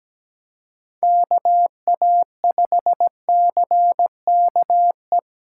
Morse code, heard as a keyed tone, about 20 wpm, 700 Hz. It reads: KA5CKE
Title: Speed Racing - USA Call Signs 17wpm